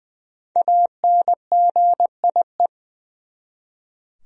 Morse code, heard as a keyed tone, about 20 wpm, 700 Hz. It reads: ANGIE
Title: USA Common Names - 20wpm